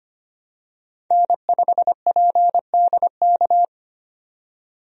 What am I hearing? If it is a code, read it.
Morse code, heard as a keyed tone, about 25 wpm, 700 Hz. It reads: N5PDK
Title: USA Call Signs - 5 Characters - 25wpm